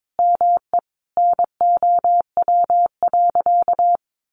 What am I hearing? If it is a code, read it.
Morse code, heard as a keyed tone, about 22 wpm, 700 Hz. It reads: ME NOW.
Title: Sentences from Top 1500 Words 22wpm